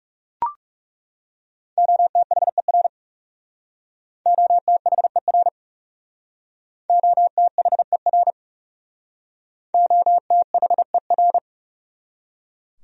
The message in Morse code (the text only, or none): OTHER OTHER OTHER OTHER